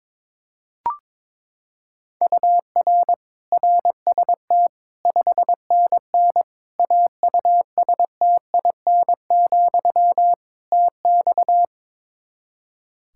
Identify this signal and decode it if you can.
Morse code, heard as a keyed tone, about 22 wpm, 700 Hz. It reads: UR RST 5NN AUSTIN, TX
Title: Practice QSO - Exchange 1 - Beginning - 22wpm